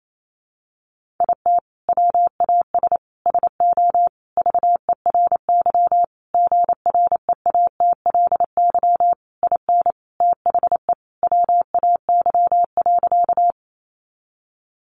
Morse code, heard as a keyed tone, about 28 wpm, 700 Hz. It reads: IT WAS SO VERY GREATLY IN THE WAY.
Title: Sentences from Top 1400 Words 28wpm